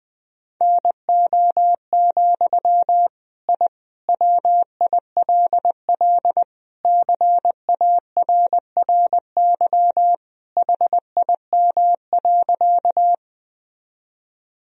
Morse code, heard as a keyed tone, about 20 wpm, 700 Hz. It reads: NO, I WILL CARRY HIM.
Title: Sentences from Top 1000 Words - Review Part 1 - 20wpm